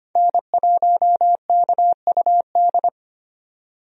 Morse code, heard as a keyed tone, about 25 wpm, 700 Hz. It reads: N1KUD
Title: USA Call Signs - 5 Characters - 25wpm